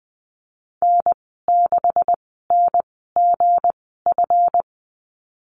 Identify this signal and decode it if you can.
Morse code, heard as a keyed tone, about 20 wpm, spaced as for 15 wpm, 700 Hz. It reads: N6NGF
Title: 231 Lesson 37g - Letter x - Call Signs 20wpm